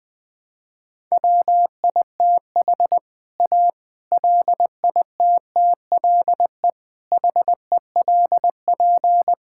Morse code, heard as a keyed tone, about 20 wpm, 700 Hz. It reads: WITH A LITTLE HELP
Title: Sentences with 7-Word Phrases - 20wpm